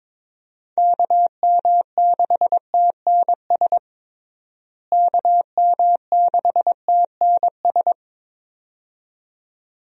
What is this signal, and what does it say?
Morse code, heard as a keyed tone, about 21 wpm, 700 Hz. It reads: KM6TNS KM6TNS
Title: USA Call Signs - 6 Characters - Repeated 2x - 22wpm